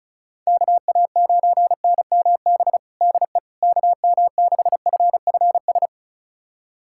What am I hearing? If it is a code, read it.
Morse code, heard as a keyed tone, about 34 wpm, 700 Hz. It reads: KA9NMB DE KM6FFS
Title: USA Callsign DE Callsign 35wpm